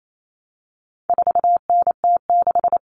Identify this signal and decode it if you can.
Morse code, heard as a keyed tone, about 28 wpm, 700 Hz. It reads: 4NT6
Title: Common Cut Numbers - 4-Digit 28wpm